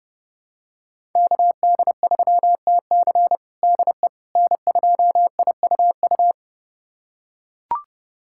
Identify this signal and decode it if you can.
Morse code, heard as a keyed tone, about 30 wpm, 700 Hz. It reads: KD3TC DE N2IUU
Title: USA Callsign DE Callsign - 5 Characters - 30wpm